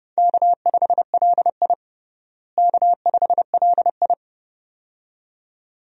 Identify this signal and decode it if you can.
Morse code, heard as a keyed tone, about 29 wpm, 700 Hz. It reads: K5LI K5LI
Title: USA Call Signs - 4 Characters - Repeated 2x - 30wpm